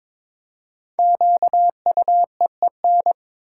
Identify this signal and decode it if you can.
Morse code, heard as a keyed tone, about 22 wpm, 700 Hz. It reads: QUEEN
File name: 5 Letter Words - First 3 Letters Given - 22wpm